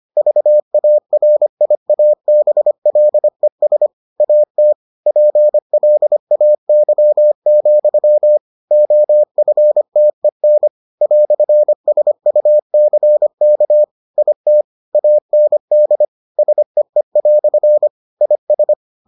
Morse code, heard as a keyed tone, about 25 wpm, 600 Hz. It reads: VARIABLES AT PLAY, OFTEN "SUCK IT AND SEE" IS